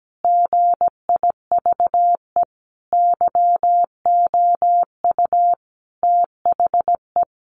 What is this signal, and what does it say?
Morse code, heard as a keyed tone, about 17 wpm, 700 Hz. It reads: GIVE YOU THE
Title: A Sound Of Thunder 17wpm